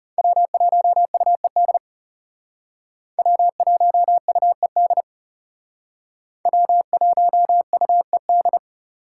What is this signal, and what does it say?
Morse code, heard as a keyed tone, about 36 wpm, 700 Hz. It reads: W1UED W1UED W1UED
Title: Speed Racing - USA 1x3 Call Signs 30wpm